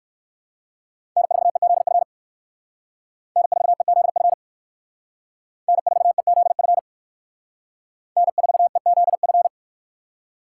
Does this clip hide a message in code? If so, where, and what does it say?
Yes: Morse code from 1.2 to 9.5 s, heard as a keyed tone, about 57 wpm, 700 Hz. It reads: N4EZF N4EZF N4EZF N4EZF